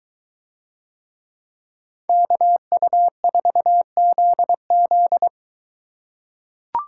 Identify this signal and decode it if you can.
Morse code, heard as a keyed tone, about 23 wpm, 700 Hz. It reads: KU4ZZ